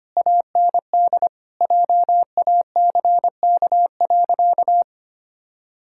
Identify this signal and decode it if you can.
Morse code, heard as a keyed tone, about 25 wpm, 700 Hz. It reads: AND JACK.
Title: Pangram Challenge v2 - 25wpm